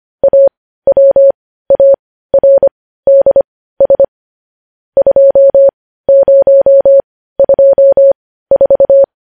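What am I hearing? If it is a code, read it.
Morse code, heard as a keyed tone, about 25 wpm, spaced as for 15 wpm, 550 Hz. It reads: AWARDS 2024
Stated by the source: Morse practice of news headlines at 15 wpm.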